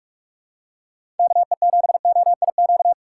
Morse code, heard as a keyed tone, about 47 wpm, 700 Hz. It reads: KE7OIQ
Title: USA Call Signs - 6 Characters - 3x Character Spacing - 45wpm